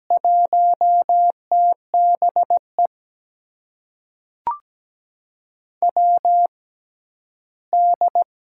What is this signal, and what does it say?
Morse code, heard as a keyed tone, about 17 wpm, spaced as for 11 wpm, 700 Hz. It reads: 1TBE WD
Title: USA Call Signs - 6 Characters - 6x Character Spacing - 17wpm